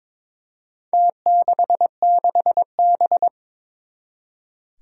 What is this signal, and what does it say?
Morse code, heard as a keyed tone, about 22 wpm, 700 Hz. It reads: T66B
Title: Four Challenging Letter-Number Combinations 22wpm